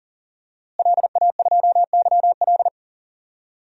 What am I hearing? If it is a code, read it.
Morse code, heard as a keyed tone, about 40 wpm, 700 Hz. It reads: LA2YL
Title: DXCC-100 Call Signs - 5 Characters - Repeated 2x - 40wpm